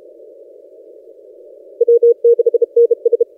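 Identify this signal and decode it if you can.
Morse code, heard as a keyed tone, about 32 wpm, 470 Hz. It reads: W6NS
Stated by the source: H44NT DX station worked on 15 m CW, recorded by N6ML.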